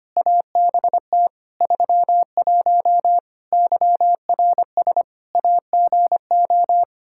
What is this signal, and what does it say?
Morse code, heard as a keyed tone, about 25 wpm, 700 Hz. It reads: ABT 31 YRS AGO
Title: K1USN SST QSO Practice 25wpm